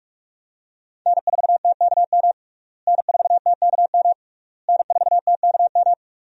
Morse code, heard as a keyed tone, about 46 wpm, 700 Hz. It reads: N4TKM N4TKM N4TKM
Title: USA Call Signs - 5 Characters - Repeated 3x - 45wpm